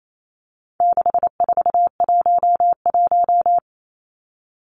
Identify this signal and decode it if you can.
Morse code, heard as a keyed tone, about 28 wpm, 700 Hz. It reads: 6411
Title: Numbers - 4-Digit 28wpm